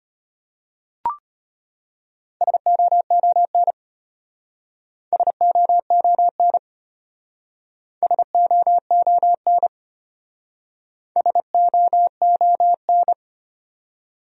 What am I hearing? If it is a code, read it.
Morse code, heard as a keyed tone, about 34 wpm, 700 Hz. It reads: SOON SOON SOON SOON